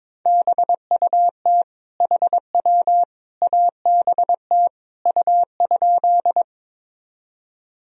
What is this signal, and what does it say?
Morse code, heard as a keyed tone, about 21 wpm, 700 Hz. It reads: BUT HW ABT U?
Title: Sentences from Top 100 Words and Abbreviations 22wpm